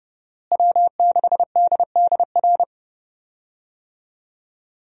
Morse code, heard as a keyed tone, about 30 wpm, 700 Hz. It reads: W6DDR
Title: USA Call Signs - 5 Characters - Repeated 3x - 30wpm